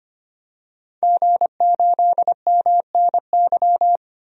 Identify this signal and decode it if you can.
Morse code, heard as a keyed tone, about 25 wpm, 700 Hz. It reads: G8MNY
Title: DXCC-100 Call Signs - 5 Characters - 25wpm